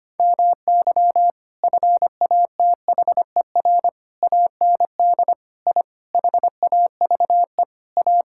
Morse code, heard as a keyed tone, about 25 wpm, 700 Hz. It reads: MY FATHER AND I HAVE A
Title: Sentences from Top 1000 Words - Review Part 6 - 25wpm